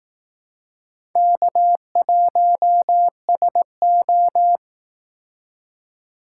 Morse code, heard as a keyed tone, about 18 wpm, 700 Hz. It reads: K1SO
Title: Speed Racing - USA 1x2 Call Signs 15wpm